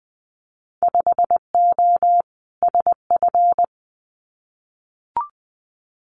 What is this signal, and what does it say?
Morse code, heard as a keyed tone, about 20 wpm, 700 Hz. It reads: 5O SF
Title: Winter Field Day Exchanges Encore 20wpm